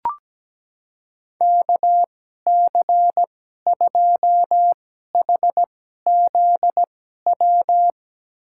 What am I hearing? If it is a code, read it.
Morse code, heard as a keyed tone, about 17 wpm, spaced as for 12 wpm, 700 Hz. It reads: KC2HZW
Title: USA Call Signs - 2x Character Spacing - 17wpm